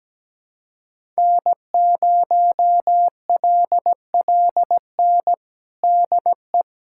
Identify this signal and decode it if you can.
Morse code, heard as a keyed tone, about 17 wpm, 700 Hz. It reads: N0LLN DE
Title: USA Callsign DE Callsign - 5 Characters - 17wpm